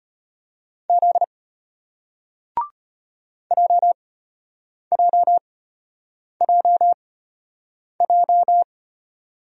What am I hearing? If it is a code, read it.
Morse code, heard as a keyed tone, about 34 wpm, spaced as for 8 wpm, 700 Hz. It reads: Z JJJJ